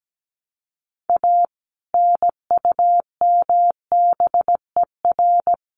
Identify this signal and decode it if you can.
Morse code, heard as a keyed tone, about 17 wpm, 700 Hz. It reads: A NUMBER
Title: Sets of 3 Words 17wpm